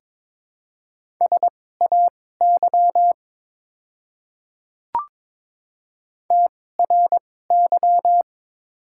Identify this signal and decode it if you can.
Morse code, heard as a keyed tone, about 22 wpm, spaced as for 16 wpm, 700 Hz. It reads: SAY TRY
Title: 2x Character Spacing - Top 200 Words - 22wpm